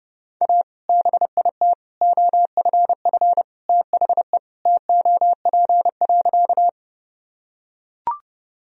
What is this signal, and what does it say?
Morse code, heard as a keyed tone, about 32 wpm, 700 Hz. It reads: A BIT OFF THE TOP.
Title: Sentences from Top 1000 Words - Review Part 2 - 30wpm